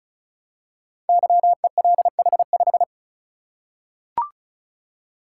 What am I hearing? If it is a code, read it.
Morse code, heard as a keyed tone, about 36 wpm, 700 Hz. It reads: YELH5
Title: Code Groups - 5 Letter:Numbers - 35wpm